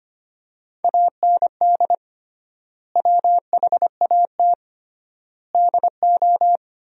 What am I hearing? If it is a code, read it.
Morse code, heard as a keyed tone, about 25 wpm, 700 Hz. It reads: AND WHAT DO
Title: 3x Word-Spacing - Sentences from Top 200 Words - 25wpm